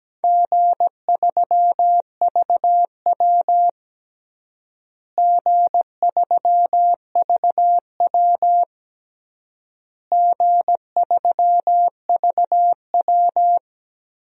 Morse code, heard as a keyed tone, about 17 wpm, 700 Hz. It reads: G3VW G3VW G3VW
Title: DXCC-100 Call Signs - 4 Characters - Repeated 3x - 17wpm